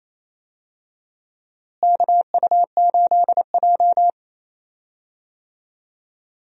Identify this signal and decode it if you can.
Morse code, heard as a keyed tone, about 28 wpm, 700 Hz. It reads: KU8J
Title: USA Call Signs - 4 Characters - Repeated 2x - 28wpm